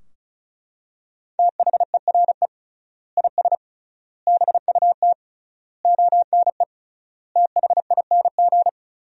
Morse code, heard as a keyed tone, about 34 wpm, 700 Hz. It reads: THERE IS BUT ONE THING
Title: 3x Word-Spacing - Sentences from Top 100 Words - 35wpm